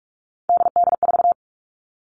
Morse code, heard as a keyed tone, about 46 wpm, 700 Hz. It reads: DD4
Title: Three Challenging Letter-Number Combinations 45wpm